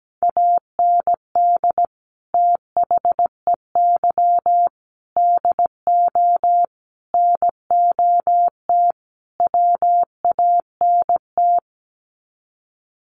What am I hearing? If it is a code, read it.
Morse code, heard as a keyed tone, about 17 wpm, 700 Hz. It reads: AND THEY DO NOT WANT
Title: Sets of 5 Words Encore2 17wpm